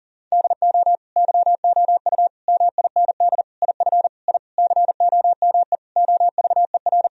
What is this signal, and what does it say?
Morse code, heard as a keyed tone, about 40 wpm, 700 Hz. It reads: DO YOU MIND IF I COME OVER
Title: Sentences from Top 1000 Words - Review Part 2 - 40wpm